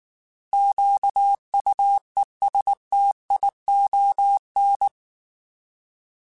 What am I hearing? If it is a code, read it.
Morse code, heard as a keyed tone, about 19 wpm, 800 Hz. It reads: QUESTION
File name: Alphabetical / Alphabetical mp3s / Qs @18